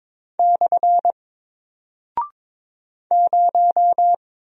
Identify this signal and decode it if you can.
Morse code, heard as a keyed tone, about 22 wpm, 700 Hz. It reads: / 0
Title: Single Character (Letter:Number:Punctuation:Prosign) Alt 22wpm